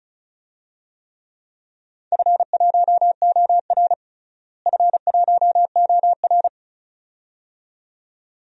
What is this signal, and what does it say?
Morse code, heard as a keyed tone, about 36 wpm, 700 Hz. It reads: F1OR F1OR
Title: DXCC-100 Call Signs - 4 Characters - Repeated 2x - 35wpm